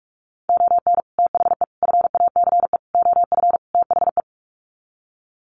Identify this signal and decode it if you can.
Morse code, heard as a keyed tone, about 44 wpm, 700 Hz. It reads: ON THE FACE OF THE
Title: Sets of 5 Words Encore2 45wpm